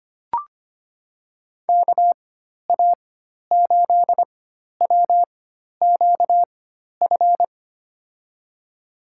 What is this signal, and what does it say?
Morse code, heard as a keyed tone, about 25 wpm, spaced as for 12 wpm, 700 Hz. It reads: KA8WQF
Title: USA Call Signs - 6 Characters - 4x Character Spacing - 25wpm